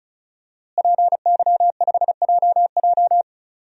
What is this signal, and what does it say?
Morse code, heard as a keyed tone, about 36 wpm, 700 Hz. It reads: PY5JJ
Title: DXCC-100 Call Signs - 5 Characters - Repeated 2x - 35wpm